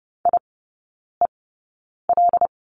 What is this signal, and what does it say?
Morse code, heard as a keyed tone, about 30 wpm, spaced as for 9 wpm, 700 Hz. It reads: IEL
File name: Lesson 16e - Letter c - Three Characters 30wpm